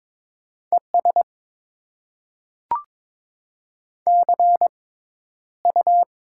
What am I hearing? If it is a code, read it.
Morse code, heard as a keyed tone, about 22 wpm, spaced as for 11 wpm, 690 Hz. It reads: ES CU
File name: Top 20 QSO Elements - 6x Character Spacing - 22wpm